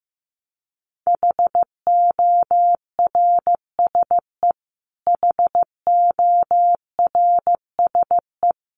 Morse code, heard as a keyed tone, about 15 wpm, 700 Hz. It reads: HORSE HORSE
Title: Top 500 Words Repeated 15wpm